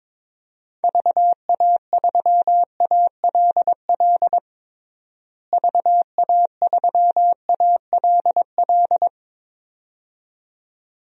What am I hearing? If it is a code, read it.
Morse code, heard as a keyed tone, about 22 wpm, 700 Hz. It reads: VA3ALL VA3ALL
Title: DXCC-100 Call Signs - 6 Characters - Repeated 2x - 22wpm